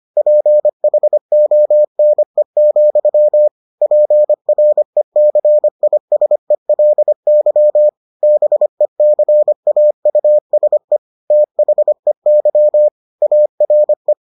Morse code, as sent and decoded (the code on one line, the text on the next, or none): .--. .... --- -. --..-- / .--. .-. -.-. .. ... .-.. -.-- / -... -.-. .- ..- ... / - .... -.-- / .- .-. .
PHONE, PRECISELY BECAUSE THEY ARE